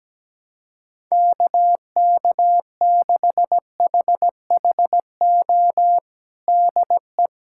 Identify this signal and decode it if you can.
Morse code, heard as a keyed tone, about 17 wpm, 700 Hz. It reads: KK6HHO DE
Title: USA Callsign DE Callsign 17wpm